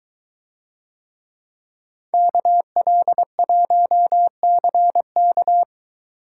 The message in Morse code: KL1CK

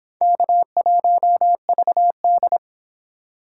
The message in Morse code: K1VD

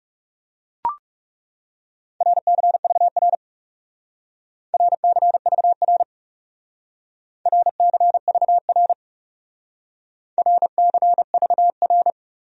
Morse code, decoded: RCVR RCVR RCVR RCVR

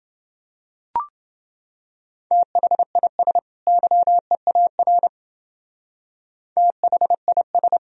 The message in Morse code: THIS YEAR THIS

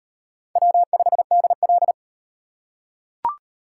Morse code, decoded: W5DL

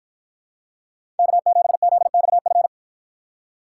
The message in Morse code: K7ZXF

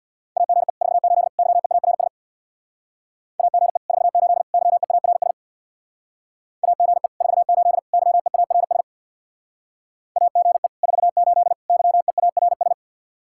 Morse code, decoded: AGE 48 YEARS AGE 48 YEARS AGE 48 YEARS AGE 48 YEARS

